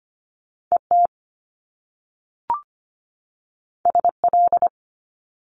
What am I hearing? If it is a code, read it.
Morse code, heard as a keyed tone, about 25 wpm, 700 Hz. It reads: ET SL
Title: Top Two Letter Combinations in Words 25wpm